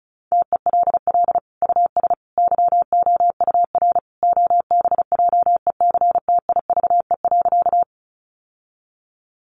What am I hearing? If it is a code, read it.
Morse code, heard as a keyed tone, about 34 wpm, 700 Hz. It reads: TELL US YOUR OBJECTIVE.
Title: Sentences from Top 1200 Words 35wpm